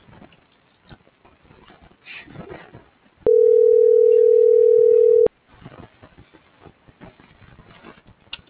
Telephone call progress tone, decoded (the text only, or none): ringback tone